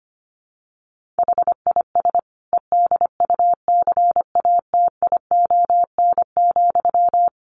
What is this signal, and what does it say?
Morse code, heard as a keyed tone, about 25 wpm, 700 Hz. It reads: HIS EDUCATION,
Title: Sentences from Top 1400 Words 25wpm